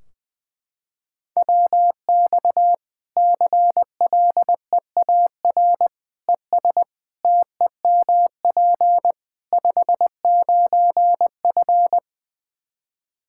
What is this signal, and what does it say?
Morse code, heard as a keyed tone, about 20 wpm, 700 Hz. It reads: WX CLEAR ES TEMP 59F
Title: K1USN SST QSO Practice 20wpm